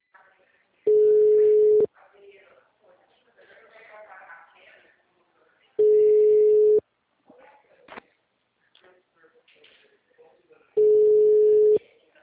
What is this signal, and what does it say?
Ringback tone (Europe, 425 Hz), with speech.